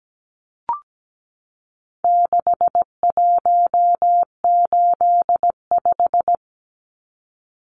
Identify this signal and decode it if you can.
Morse code, heard as a keyed tone, about 17 wpm, 700 Hz. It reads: 6185
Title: Numbers - 4-Digit 17wpm